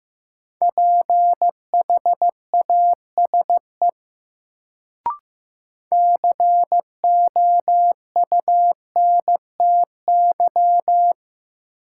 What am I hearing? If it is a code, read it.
Morse code, heard as a keyed tone, about 15 wpm, 700 Hz. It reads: PHASE COUNTY
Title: Sentences from Top 1200 Words - New Word Drill - 15wpm